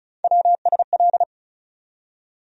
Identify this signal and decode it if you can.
Morse code, heard as a keyed tone, about 34 wpm, 700 Hz. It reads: WSL
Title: USA Call Sign Postfixes - 3 Letters 35wpm